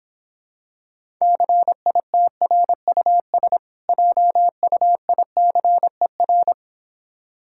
Morse code, heard as a keyed tone, about 26 wpm, 700 Hz. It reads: CITRUS JUICER